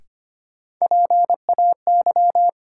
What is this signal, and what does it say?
Morse code, heard as a keyed tone, about 25 wpm, 700 Hz. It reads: PAY
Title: Sentences from Top 700 Words - New Word Drill - 25wpm